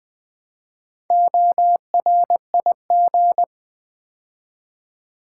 Morse code, heard as a keyed tone, about 20 wpm, 700 Hz. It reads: ORIG
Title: Build A Word - 2 Letters At A Time - 8-Letter Words - 20wpm